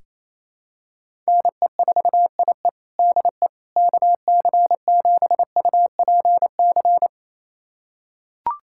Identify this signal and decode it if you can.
Morse code, heard as a keyed tone, about 28 wpm, 700 Hz. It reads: NE4IE DE KC7UPC
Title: USA Callsign DE Callsign 28wpm